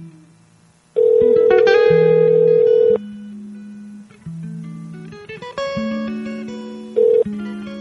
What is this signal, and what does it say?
Ringback tone (North America), 440 + 480 Hz.
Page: Madhouse Radio Prank Calls